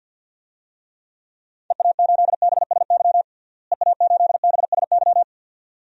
Morse code, heard as a keyed tone, about 51 wpm, 700 Hz. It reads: EA8BSY EA8BSY
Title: DXCC-100 Call Signs - 6 Characters - Repeated 2x - 50wpm